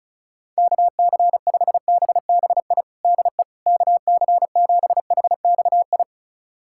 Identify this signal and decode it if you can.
Morse code, heard as a keyed tone, about 36 wpm, 700 Hz. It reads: KC5BBI DE KC7HXI
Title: USA Callsign DE Callsign - 6 Characters - 35wpm